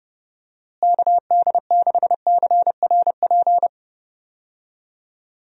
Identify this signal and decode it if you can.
Morse code, heard as a keyed tone, about 30 wpm, 700 Hz. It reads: KD6CRP
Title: USA Call Signs - 6 Characters - Repeated 3x - 30wpm